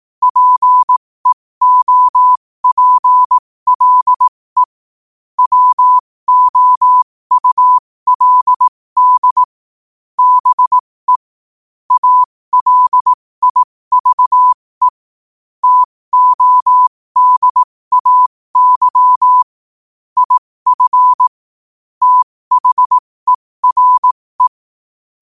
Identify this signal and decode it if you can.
Morse code, heard as a keyed tone, about 18 wpm, spaced as for 15 wpm, 1000 Hz. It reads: PEOPLE WOULD BE ALIVE TODAY IF THERE
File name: qotdfiles / 15wpm